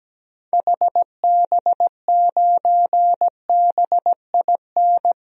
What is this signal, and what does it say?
Morse code, heard as a keyed tone, about 17 wpm, 700 Hz. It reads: HB9BIN
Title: DXCC-100 Call Signs - 6 Characters - 17wpm